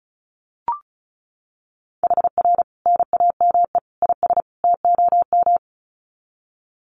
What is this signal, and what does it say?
Morse code, heard as a keyed tone, about 34 wpm, 700 Hz. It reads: HR NAME IS TOM
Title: Common QSO Phrases 35wpm